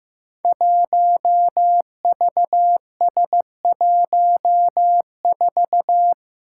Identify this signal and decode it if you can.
Morse code, heard as a keyed tone, about 15 wpm, 700 Hz. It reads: 1VS14
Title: Five Challenging Letter-Number Combinations 15wpm